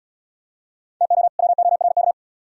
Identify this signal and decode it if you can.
Morse code, heard as a keyed tone, about 75 wpm, 700 Hz. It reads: TO WORK